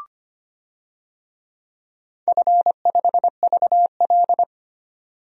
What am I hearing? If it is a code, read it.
Morse code, heard as a keyed tone, about 25 wpm, 690 Hz. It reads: F5VL
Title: DXCC-100 Call Signs - 4 Characters - 25wpm